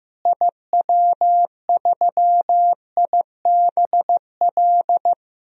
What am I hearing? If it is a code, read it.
Morse code, heard as a keyed tone, about 15 wpm, 700 Hz. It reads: IW3IBL
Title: DXCC-100 Call Signs - 6 Characters - Repeated 3x - 15wpm